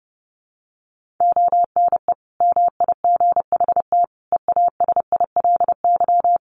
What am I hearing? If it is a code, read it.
Morse code, heard as a keyed tone, about 30 wpm, 700 Hz. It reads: ONE MIGHT EASILY
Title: Sentences from Top 1400 Words 30wpm